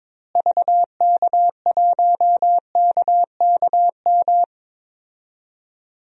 Morse code, heard as a keyed tone, about 22 wpm, 700 Hz. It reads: VK1KKM
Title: DXCC-100 Call Signs - 6 Characters - Repeated 3x - 22wpm